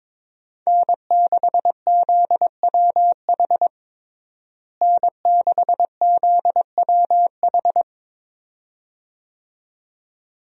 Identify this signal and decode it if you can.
Morse code, heard as a keyed tone, about 22 wpm, 700 Hz. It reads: N6ZWH N6ZWH
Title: USA Call Signs - 5 Characters - Repeated 2x - 22wpm